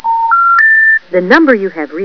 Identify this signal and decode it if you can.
Special information tone, with speech.